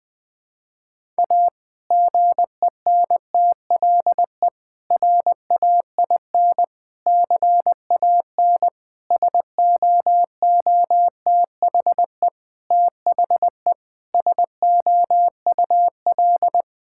Morse code, heard as a keyed tone, about 20 wpm, 700 Hz. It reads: A GENTLE RAIN CAN SOOTHE THE SOUL
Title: Difficult Sentences with Hints - 20wpm